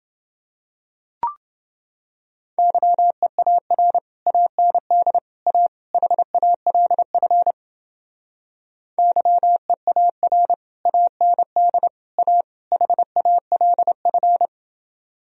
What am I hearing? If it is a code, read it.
Morse code, heard as a keyed tone, about 30 wpm, 700 Hz. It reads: YEAR AND A HALF YEAR AND A HALF